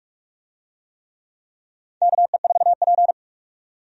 Morse code, heard as a keyed tone, about 48 wpm, 700 Hz. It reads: KE4P
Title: USA Call Signs - 4 Characters - 45wpm